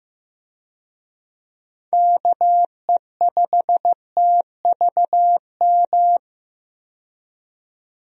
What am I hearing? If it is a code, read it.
Morse code, heard as a keyed tone, about 15 wpm, 700 Hz. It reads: KE5TVM
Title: USA Call Signs - 6 Characters - 15wpm